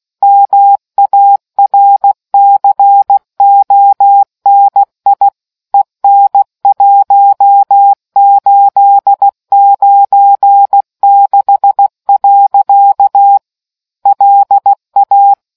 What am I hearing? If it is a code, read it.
Morse code, heard as a keyed tone, about 16 wpm, 800 Hz. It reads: MARCONI EN 1896. LA